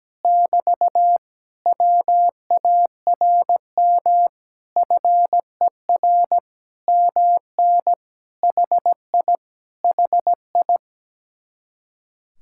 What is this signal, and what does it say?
Morse code, heard as a keyed tone, about 17 wpm, 700 Hz. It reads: <BT> WARM FER MN HI HI
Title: Speed-Racing Common QSO Phrases Encore - 17wpm